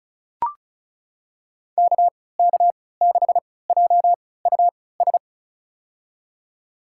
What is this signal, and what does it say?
Morse code, heard as a keyed tone, about 36 wpm, spaced as for 20 wpm, 700 Hz. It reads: KK6JUS
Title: USA Call Signs - 6 Characters - 3x Character Spacing - 35wpm